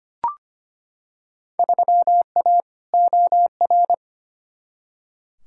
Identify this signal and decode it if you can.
Morse code, heard as a keyed tone, about 25 wpm, 700 Hz. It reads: 3A OR
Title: Field Day Exchanges Encore - 25wpm